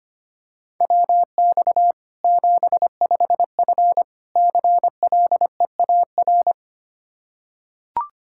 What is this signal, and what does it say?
Morse code, heard as a keyed tone, about 25 wpm, 700 Hz. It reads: WX 75F CLEAR
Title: Practice QSO - Exchange 2 - Info - 25wpm